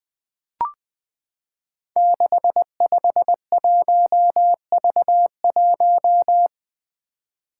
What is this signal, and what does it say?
Morse code, heard as a keyed tone, about 21 wpm, 700 Hz. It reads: 651V1
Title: Five Challenging Letter-Number Combinations 20wpm